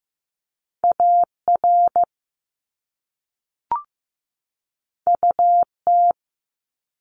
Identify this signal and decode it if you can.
Morse code, heard as a keyed tone, about 15 wpm, 700 Hz. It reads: AR UT
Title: ARRL Sections 15wpm